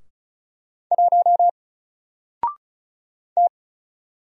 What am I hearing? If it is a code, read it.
Morse code, heard as a keyed tone, about 35 wpm, 700 Hz. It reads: 1 T
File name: Single-Letter-Number-35wpm